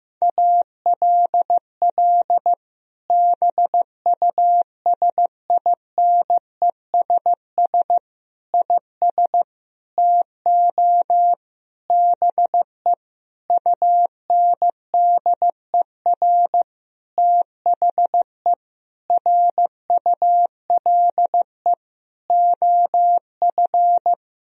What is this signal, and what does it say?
Morse code, heard as a keyed tone, about 16 wpm, 700 Hz. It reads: ALL BUSINESS IS TO BE UNDER THE RULE OF
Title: Sentences from Top 1000 Words - Review Part 3 - 15wpm